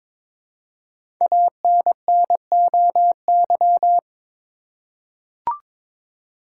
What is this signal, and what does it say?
Morse code, heard as a keyed tone, about 22 wpm, 700 Hz. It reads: ANNOY
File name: 5 Letter Words - First 3 Letters Given - 22wpm